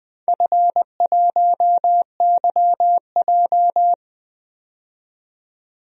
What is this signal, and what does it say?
Morse code, heard as a keyed tone, about 20 wpm, 700 Hz. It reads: F1YJ
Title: DXCC-100 Call Signs - 4 Characters - 20wpm